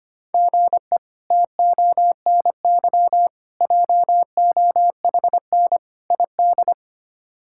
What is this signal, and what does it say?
Morse code, heard as a keyed tone, about 25 wpm, 700 Hz. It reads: GE TONY JOHN ID
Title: K1USN SST QSO Practice - 25wpm